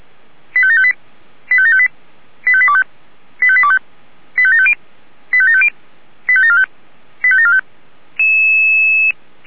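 Five-tone selective call (ZVEI), heard as quick runs of five tones, tones of about 70 ms. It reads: 85758 85758 85825 85825 85750 85750 85745 85745
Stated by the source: ZVEI five-tone selective call received by SDR.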